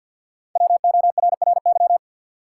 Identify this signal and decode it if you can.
Morse code, heard as a keyed tone, about 49 wpm, 700 Hz. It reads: WORRY
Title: Five Letter Words - 50wpm